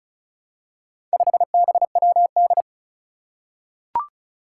Morse code, heard as a keyed tone, about 36 wpm, 700 Hz. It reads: 5BWD
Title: Four Challenging Letter-Number Combinations 35wpm